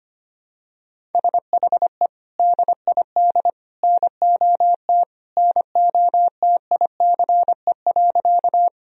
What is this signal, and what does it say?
Morse code, heard as a keyed tone, about 25 wpm, 700 Hz. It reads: SHE DID NOT NOTICE.
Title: Sentences from Top 1000 Words - Review Part 2 - 25wpm